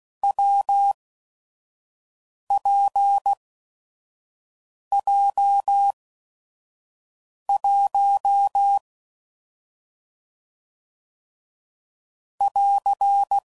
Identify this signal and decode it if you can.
Morse code, heard as a keyed tone, about 16 wpm, spaced as for 5 wpm, 800 Hz. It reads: WPJ1 <AR>
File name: Combinations / Difficult Combos / AWJ1